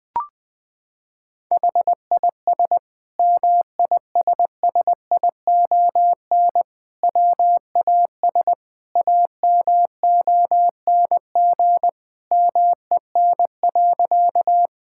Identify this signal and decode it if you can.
Morse code, heard as a keyed tone, about 21 wpm, 700 Hz. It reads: HIS MISSION WAS AMONG MEN.
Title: Sentences from Top 1600 Words 20wpm